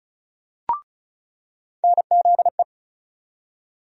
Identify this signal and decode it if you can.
Morse code, heard as a keyed tone, about 36 wpm, 700 Hz. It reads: NZE
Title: USA Call Sign Postfixes - 3 Letters 35wpm